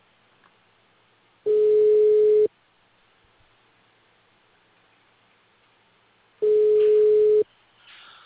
Ringback tone (Europe, 425 Hz).